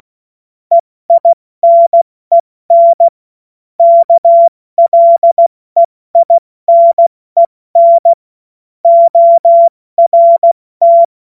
Morse code, heard as a keyed tone, about 16 wpm, 680 Hz. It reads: EINEN KLEINEN ORT
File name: Morsekurs / Witze / Witz 31. Mai 2022 70 F80